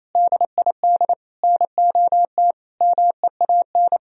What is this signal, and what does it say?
Morse code, heard as a keyed tone, about 38 wpm, 700 Hz. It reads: DID NOT MEAN